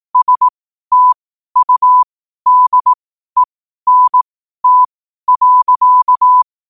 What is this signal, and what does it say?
Morse code, heard as a keyed tone, about 18 wpm, spaced as for 12 wpm, 1000 Hz. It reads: STUDENT.
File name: qotdfiles / 12wpm